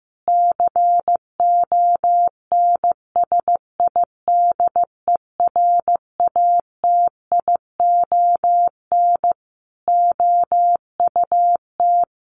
Morse code, heard as a keyed tone, about 15 wpm, 700 Hz. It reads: CONSIDERATION OUT
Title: Sentences from Top 1500 Words 15wpm